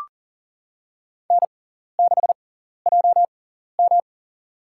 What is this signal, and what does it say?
Morse code, heard as a keyed tone, about 40 wpm, spaced as for 14 wpm, 700 Hz. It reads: N6JM
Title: USA Call Signs - 6x Character Spacing - 40wpm